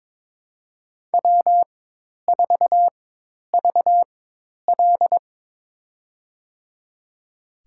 Morse code, heard as a keyed tone, about 22 wpm, spaced as for 10 wpm, 700 Hz. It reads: W4VL
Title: USA Call Signs - 4x Character Spacing - 22wpm